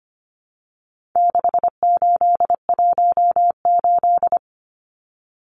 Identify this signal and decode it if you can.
Morse code, heard as a keyed tone, about 25 wpm, 700 Hz. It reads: 6818
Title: Numbers - 4-Digit 25wpm